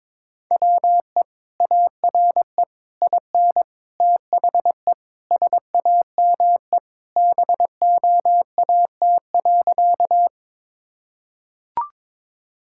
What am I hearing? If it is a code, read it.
Morse code, heard as a keyed tone, about 22 wpm, 700 Hz. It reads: WE ARE IN THE SAME BOAT.
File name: practice / 22wpm / Sentences with 3-Word Phrases - 22wpm